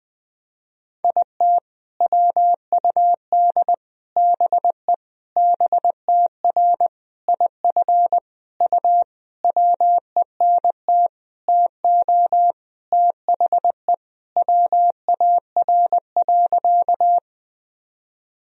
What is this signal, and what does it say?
Morse code, heard as a keyed tone, about 20 wpm, 700 Hz. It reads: IT WUD BE BTR IF U WENT TO THE WAR.
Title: Sentences from Top 400 Words and Abbreviations 20wpm